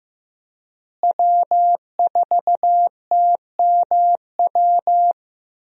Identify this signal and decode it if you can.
Morse code, heard as a keyed tone, about 15 wpm, 700 Hz. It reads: W4TMW
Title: USA Call Signs - 5 Characters - Repeated 2x - 15wpm